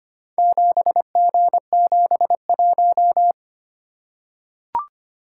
Morse code, heard as a keyed tone, about 25 wpm, 700 Hz. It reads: 7G71
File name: practice / 25wpm / Four Letter-Number Combinations - 25wpm